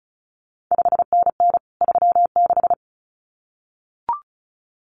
Morse code, heard as a keyed tone, about 36 wpm, 700 Hz. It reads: 5NN 36
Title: CQ WW DX Contest Exchange Encore - 35wpm